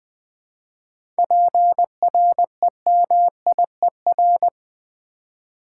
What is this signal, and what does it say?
Morse code, heard as a keyed tone, about 20 wpm, 700 Hz. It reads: PREMIER
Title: 7 Letter Words - First 3 Letters Given - 20wpm